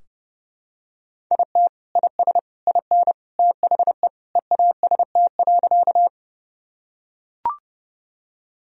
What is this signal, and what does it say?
Morse code, heard as a keyed tone, about 30 wpm, 700 Hz. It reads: IT IS IN THE EAST.
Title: Sentences from Top 1000 Words - Review Part 3 - 30wpm